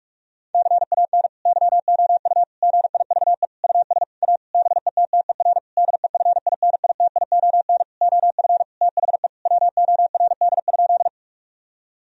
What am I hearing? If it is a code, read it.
Morse code, heard as a keyed tone, about 46 wpm, 700 Hz. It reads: CAN YOU GIVE US A BETTER DEFINITION OF THE WORD?
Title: Sentences from Top 1000 Words - Review Part 4 - 45wpm